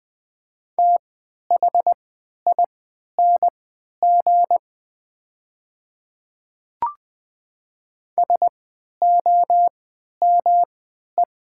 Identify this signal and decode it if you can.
Morse code, heard as a keyed tone, about 20 wpm, spaced as for 11 wpm, 700 Hz. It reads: THING SOME
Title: 3x Character Spacing - Top 200 Words - 20wpm